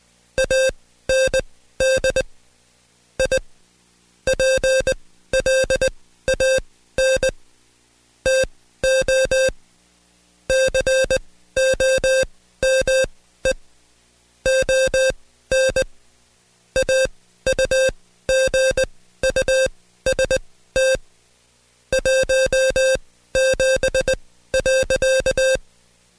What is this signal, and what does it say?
Morse code, heard as a keyed tone, about 20 wpm, spaced as for 14 wpm, 530 Hz. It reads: AND I PLAN TO COME ON AUGUST 17.
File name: Code Quick / AudioDownld2 / Seg133-Stepper6 13WPM